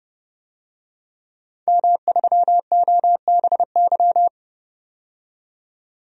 Morse code, heard as a keyed tone, about 32 wpm, 700 Hz. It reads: M3OBY
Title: DXCC-100 Call Signs - 5 Characters - 30wpm